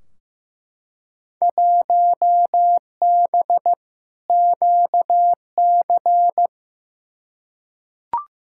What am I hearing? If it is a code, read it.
Morse code, heard as a keyed tone, about 15 wpm, 700 Hz. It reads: 1B QC
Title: Field Day Exchanges Encore - 15wpm